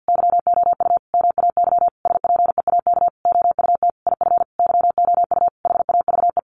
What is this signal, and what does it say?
Morse code, heard as a keyed tone, about 50 wpm, 700 Hz. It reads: YOU MAY SPEAK OUT IF YOU HAVE
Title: Sentences from Top 1300 Words 50wpm